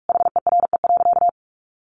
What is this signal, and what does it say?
Morse code, heard as a keyed tone, about 44 wpm, 700 Hz. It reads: HERE.
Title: Sentences from Top 1100 Words 45wpm